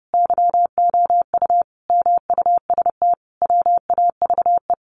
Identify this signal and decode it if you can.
Morse code, heard as a keyed tone, about 30 wpm, 700 Hz. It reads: YOU MUST WAVE
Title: Sentences from Top 1400 Words 30wpm